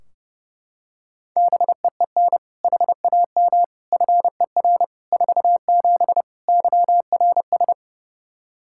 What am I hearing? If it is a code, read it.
Morse code, heard as a keyed tone, about 32 wpm, 700 Hz. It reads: BEEN HAM FER 47 YRS
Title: Practice QSO - Exchange 3 - More Info - 30wpm